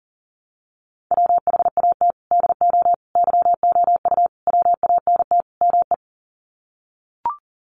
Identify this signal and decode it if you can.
Morse code, heard as a keyed tone, about 38 wpm, 700 Hz. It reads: WHAT DO YOU WANT ME
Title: Sets of 5 Words Encore2 40wpm